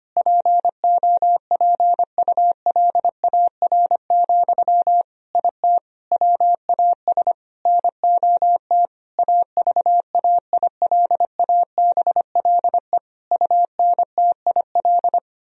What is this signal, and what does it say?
Morse code, heard as a keyed tone, about 25 wpm, 700 Hz. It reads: POPULAR, IT WAS NOT AVAILABLE UNTIL
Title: Sentences from Top 1000 Words - Review Part 3 - 25wpm